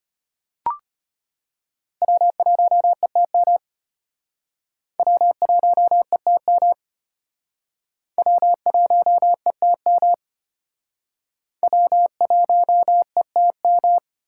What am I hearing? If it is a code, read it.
Morse code, heard as a keyed tone, about 33 wpm, 700 Hz. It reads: W1ETM W1ETM W1ETM W1ETM